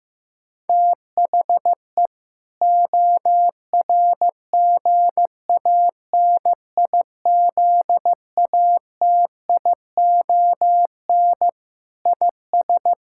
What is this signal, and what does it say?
Morse code, heard as a keyed tone, about 15 wpm, 700 Hz. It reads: THE ORGANIZATION IS
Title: Sentences from Top 1000 Words - Review Part 5 - 15wpm